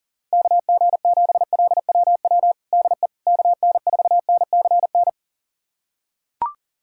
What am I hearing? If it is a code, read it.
Morse code, heard as a keyed tone, about 40 wpm, 700 Hz. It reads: KG7LWW DE KN4NCN
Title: USA Callsign DE Callsign 40wpm